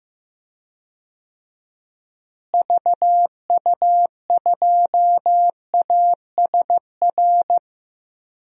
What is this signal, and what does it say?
Morse code, heard as a keyed tone, about 15 wpm, 700 Hz. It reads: VU2ASR
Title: DXCC-100 Call Signs - 6 Characters - Repeated 2x - 15wpm